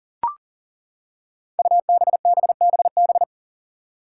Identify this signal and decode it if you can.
Morse code, heard as a keyed tone, about 40 wpm, 700 Hz. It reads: UBBBB
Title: Five Challenging Letter-Number Combinations 40wpm